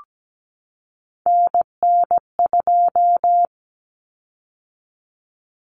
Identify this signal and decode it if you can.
Morse code, heard as a keyed tone, about 17 wpm, 700 Hz. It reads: NN2
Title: Common Cut Numbers - 3-Digit 17wpm